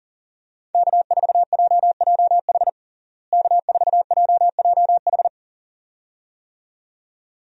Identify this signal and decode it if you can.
Morse code, heard as a keyed tone, about 37 wpm, 700 Hz. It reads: K4JJH K4JJH
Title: USA Call Signs - 5 Characters - Repeated 2x - 40wpm